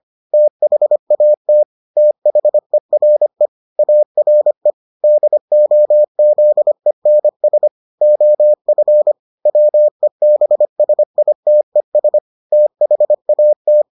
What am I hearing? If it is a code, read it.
Morse code, heard as a keyed tone, about 25 wpm, 600 Hz. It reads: THAT THERE ARE DOZENS OF WEBSITES THAT